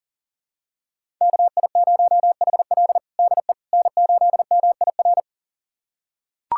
Morse code, heard as a keyed tone, about 40 wpm, 700 Hz. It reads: KI0HL DE N8MIR
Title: USA Callsign DE Callsign - 5 Characters - 40wpm